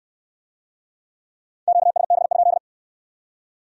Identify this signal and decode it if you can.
Morse code, heard as a keyed tone, about 67 wpm, 700 Hz. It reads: QSB?